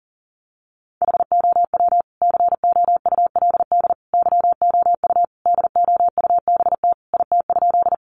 Morse code, heard as a keyed tone, about 40 wpm, 700 Hz. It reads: HOW COULD YOU DOUBT IT?
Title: Sentences from Top 1500 Words 40wpm